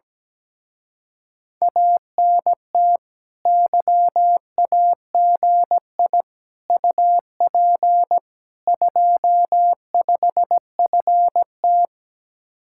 Morse code, heard as a keyed tone, about 17 wpm, 700 Hz. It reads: ANT YAGI UP 25FT
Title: Speed-Racing Common QSO Phrases - 17wpm